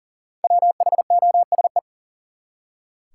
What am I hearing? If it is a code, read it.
Morse code, heard as a keyed tone, about 40 wpm, 700 Hz. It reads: WHOSE
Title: Five Letter Words - 40wpm